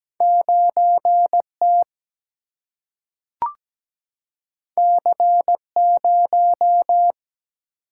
Morse code, heard as a keyed tone, about 17 wpm, 700 Hz. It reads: 9 T C 0
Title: Single-Letter Single-Number 17wpm